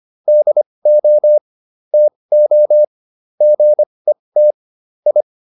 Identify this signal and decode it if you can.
Morse code, heard as a keyed tone, about 25 wpm, spaced as for 20 wpm, 600 Hz. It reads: DO TO GET I